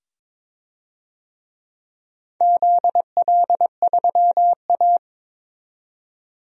Morse code, heard as a keyed tone, about 22 wpm, 700 Hz. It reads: ZL3A
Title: DXCC-100 Call Signs - 4 Characters - 22wpm